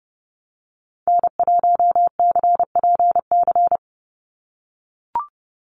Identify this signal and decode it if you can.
Morse code, heard as a keyed tone, about 30 wpm, 700 Hz. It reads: N1CPC
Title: USA Call Signs 30wpm